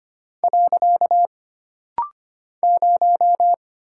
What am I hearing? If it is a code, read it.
Morse code, heard as a keyed tone, about 25 wpm, 700 Hz. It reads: . 0
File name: Single Character (Letter:Number:Punctuation:Prosign) - Rapid-Fire - 25wpm